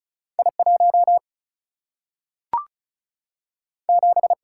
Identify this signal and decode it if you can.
Morse code, heard as a keyed tone, about 36 wpm, 700 Hz. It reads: I 1 7
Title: Single-Letter Single-Number 35wpm